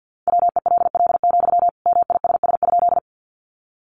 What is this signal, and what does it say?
Morse code, heard as a keyed tone, about 50 wpm, 700 Hz. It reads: WELL, MISS?
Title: Sentences from Top 500 Words 50wpm